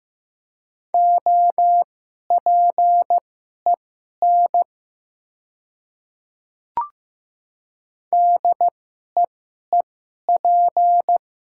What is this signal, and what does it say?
Morse code, heard as a keyed tone, about 15 wpm, spaced as for 11 wpm, 700 Hz. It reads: OPEN DEEP
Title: 2x Character Spacing - Top 500 Words - 15wpm